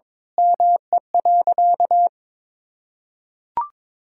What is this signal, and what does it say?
Morse code, heard as a keyed tone, about 22 wpm, 700 Hz. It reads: ME.
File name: practice / 22wpm / Sentences with 8-Word Phrases - 22wpm